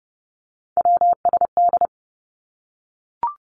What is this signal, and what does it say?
Morse code, heard as a keyed tone, about 30 wpm, 700 Hz. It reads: WSD
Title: Three Challenging Letter-Number Combinations 30wpm